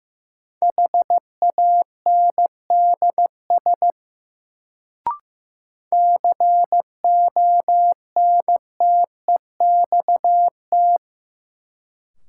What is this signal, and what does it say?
Morse code, heard as a keyed tone, about 15 wpm, 700 Hz. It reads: HANDS CONTEXT
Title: Sets of 4 Words - New Word Drill - 15wpm